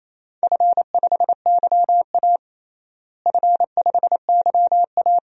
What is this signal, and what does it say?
Morse code, heard as a keyed tone, about 28 wpm, 700 Hz. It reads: F5YA F5YA
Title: DXCC-100 Call Signs - 4 Characters - Repeated 2x - 28wpm